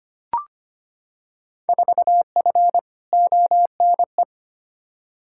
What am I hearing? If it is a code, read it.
Morse code, heard as a keyed tone, about 25 wpm, 700 Hz. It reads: 4F ONE
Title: Field Day Exchanges Encore - 25wpm